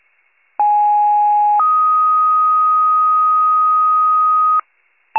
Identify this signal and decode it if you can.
Two-tone page, heard as a one-second tone, then a three-second tone, about 830 then 1250 Hz. It is heard from 0.6 to 4.6 s.